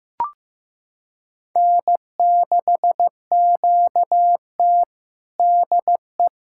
Morse code, heard as a keyed tone, about 15 wpm, 700 Hz. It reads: N6QT DE
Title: USA Callsign DE Callsign - 4 Characters - 15wpm